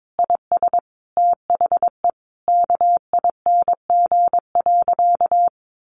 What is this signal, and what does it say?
Morse code, heard as a keyed tone, about 22 wpm, 700 Hz. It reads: IS THE KING.
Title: Sentences from Top 1200 Words 22wpm